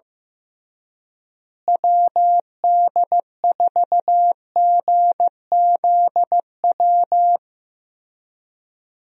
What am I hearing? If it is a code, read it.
Morse code, heard as a keyed tone, about 15 wpm, 700 Hz. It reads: WD4GZW
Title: USA Call Signs - 6 Characters - Repeated 3x - 15wpm